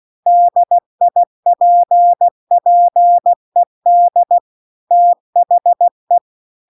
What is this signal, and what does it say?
Morse code, heard as a keyed tone, about 16 wpm, 700 Hz. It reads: DIPPED THE